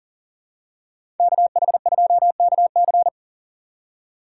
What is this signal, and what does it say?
Morse code, heard as a keyed tone, about 40 wpm, 700 Hz. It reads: KH2KC
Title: USA Call Signs - 6x Character Spacing - 40wpm